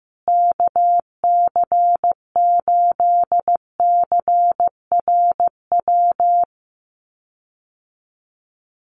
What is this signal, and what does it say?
Morse code, heard as a keyed tone, about 15 wpm, 700 Hz. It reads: KC8CRW
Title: USA Call Signs Repeated 2x 15wpm